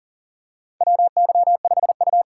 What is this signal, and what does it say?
Morse code, heard as a keyed tone, about 40 wpm, 700 Hz. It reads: WY5U
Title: USA Call Signs - 3x Character Spacing - 40wpm